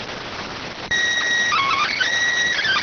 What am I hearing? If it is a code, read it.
SSTV (slow-scan television), PD 290.